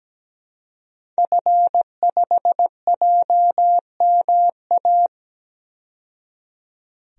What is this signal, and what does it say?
Morse code, heard as a keyed tone, about 17 wpm, 700 Hz. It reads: F5JMA
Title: DXCC-100 Call Signs - 5 Characters - 17wpm